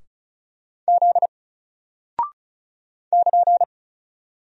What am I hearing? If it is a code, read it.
Morse code, heard as a keyed tone, about 36 wpm, 700 Hz. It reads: Z <KN>
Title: Single Character (Letter:Number:Punctuation:Prosign) Alt 35wpm